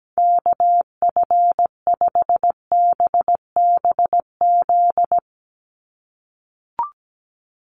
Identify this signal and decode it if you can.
Morse code, heard as a keyed tone, about 17 wpm, 700 Hz. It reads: KF5BBZ
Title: USA Call Signs 17wpm